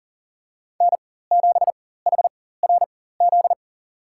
Morse code, heard as a keyed tone, about 40 wpm, spaced as for 19 wpm, 700 Hz. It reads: N7HRZ
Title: USA Call Signs - 4x Character Spacing - 40wpm